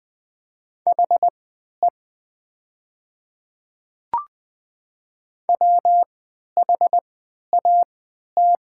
Morse code, heard as a keyed tone, about 20 wpm, spaced as for 11 wpm, 700 Hz. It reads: HE WHAT
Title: 3x Character Spacing - Top 200 Words - 20wpm